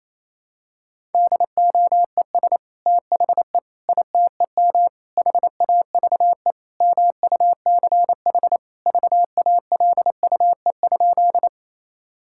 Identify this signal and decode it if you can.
Morse code, heard as a keyed tone, about 28 wpm, 700 Hz. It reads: DOES THE ITEM HAVE MUCH VALUE?
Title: Sentences from Top 1000 Words - Review Part 6 - 28wpm